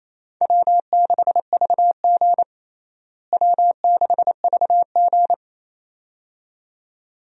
Morse code, heard as a keyed tone, about 28 wpm, 700 Hz. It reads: W6VG W6VG
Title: USA Call Signs - 4 Characters - Repeated 2x - 28wpm